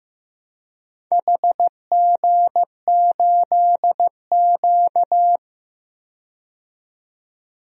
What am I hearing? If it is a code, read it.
Morse code, heard as a keyed tone, about 15 wpm, 700 Hz. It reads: HG8Q
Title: DXCC-100 Call Signs - 4 Characters - 15wpm